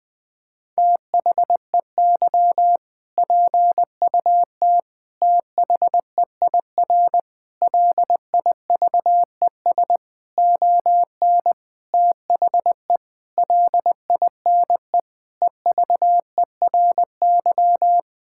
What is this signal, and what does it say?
Morse code, heard as a keyed tone, about 20 wpm, 700 Hz. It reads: THEY PUT THEIR LIVES ON THE LINE EVERY
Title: Sentences with 8-Word Phrases - 20wpm